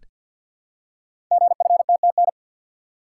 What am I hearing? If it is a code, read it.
Morse code, heard as a keyed tone, about 51 wpm, 700 Hz. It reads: GFTTN
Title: Code Groups - 5 Letters - 50wpm